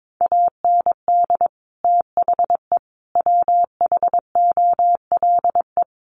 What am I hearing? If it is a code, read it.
Morse code, heard as a keyed tone, about 22 wpm, 700 Hz. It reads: AND THE WHOLE
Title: Sentences from Top 1300 Words 22wpm